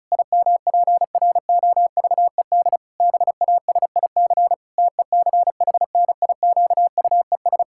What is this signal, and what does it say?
Morse code, heard as a keyed tone, about 36 wpm, 700 Hz. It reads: IMPROVED BASIC TECHNIQUES
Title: Pangram Challenge v2 - 35wpm